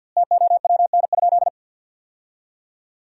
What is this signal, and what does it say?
Morse code, heard as a keyed tone, about 50 wpm, 700 Hz. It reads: TOWN?
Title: 2x Word-Spacing - Sentences from Top 500 Words - 50wpm